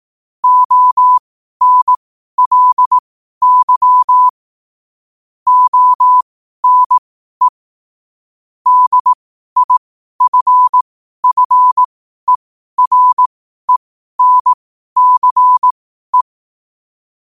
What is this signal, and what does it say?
Morse code, heard as a keyed tone, about 18 wpm, spaced as for 12 wpm, 1000 Hz. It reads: ONLY ONE DIFFERENCE
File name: qotdfiles / 12wpm